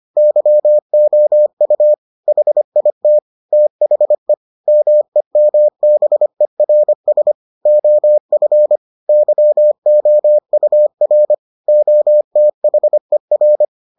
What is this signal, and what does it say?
Morse code, heard as a keyed tone, about 25 wpm, 600 Hz. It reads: YOU HIT THE MEMBERS OF YOUR OTHER